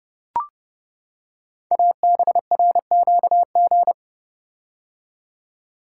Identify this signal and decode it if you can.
Morse code, heard as a keyed tone, about 28 wpm, 700 Hz. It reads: ABRQG
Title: Code Groups - 5 Letters - 30wpm